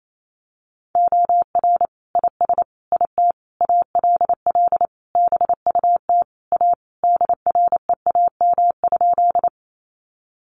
Morse code, heard as a keyed tone, about 28 wpm, 700 Hz. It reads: OR IS IT ALL BUT A DREAM?
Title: Sentences from Top 1500 Words 28wpm